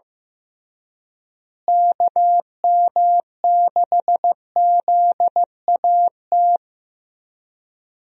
Morse code, heard as a keyed tone, about 15 wpm, 700 Hz. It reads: KM6ZAT
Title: USA Call Signs - 6 Characters - Repeated 2x - 15wpm